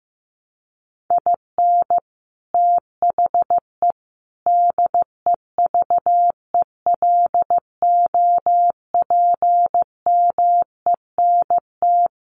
Morse code, heard as a keyed tone, about 15 wpm, 700 Hz. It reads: IN THE DEVELOPMENT
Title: Sets of 5 Words Encore2 15wpm